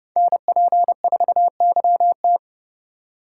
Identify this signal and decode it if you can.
Morse code, heard as a keyed tone, about 30 wpm, 700 Hz. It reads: NP4YT